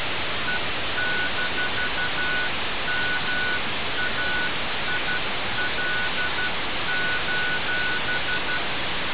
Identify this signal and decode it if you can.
Morse code, heard as a keyed tone, about 12 wpm, 1480 Hz. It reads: E-MAIL:
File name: radiocap / REC z